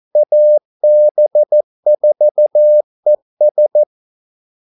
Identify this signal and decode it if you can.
Morse code, heard as a keyed tone, about 14 wpm, 600 Hz. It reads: AB4ES
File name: Callsigns-14wpm